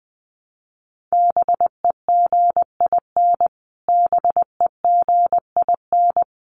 Morse code, heard as a keyed tone, about 20 wpm, 700 Hz. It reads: BEGIN BEGIN
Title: Top 500 Words Repeated 20wpm